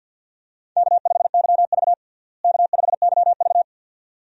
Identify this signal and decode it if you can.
Morse code, heard as a keyed tone, about 48 wpm, 700 Hz. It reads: K5YV K5YV
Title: USA Call Signs - 4 Characters - Repeated 2x - 50wpm